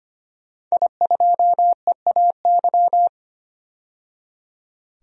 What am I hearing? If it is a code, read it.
Morse code, heard as a keyed tone, about 25 wpm, 700 Hz. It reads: I2EAY
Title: DXCC-100 Call Signs - 5 Characters - 25wpm